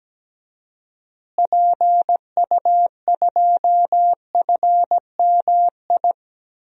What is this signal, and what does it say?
Morse code, heard as a keyed tone, about 17 wpm, 700 Hz. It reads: PU2FMI
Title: DXCC-100 Call Signs - 6 Characters - Repeated 3x - 17wpm